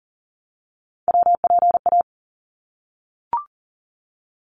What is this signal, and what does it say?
Morse code, heard as a keyed tone, about 40 wpm, 700 Hz. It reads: WPA
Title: ARRL Sections 40wpm